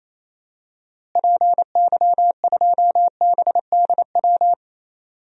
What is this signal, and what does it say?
Morse code, heard as a keyed tone, about 28 wpm, 700 Hz. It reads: PY2BDW
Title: DXCC-100 Call Signs - 6 Characters - Repeated 3x - 28wpm